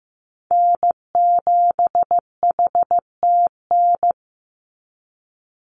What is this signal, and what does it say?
Morse code, heard as a keyed tone, about 15 wpm, 700 Hz. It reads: N7HTN
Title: USA Call Signs 15wpm